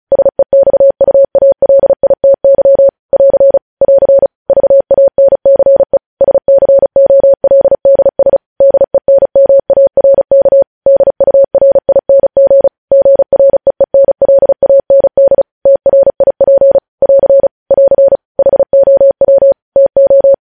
Morse code, heard as a keyed tone, about 36 wpm, 550 Hz. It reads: SEXUALITY <AR> <AR> VANCE SCOLDS DENMARK DURING GREENLAND TRIP <AR> <AR> HOW TO
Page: News headlines in Morse code at 35 wpm